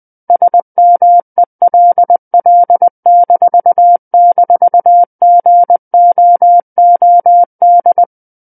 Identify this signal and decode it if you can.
Morse code, heard as a keyed tone, about 20 wpm, 700 Hz. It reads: SMELL--GOOD
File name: 20 WPM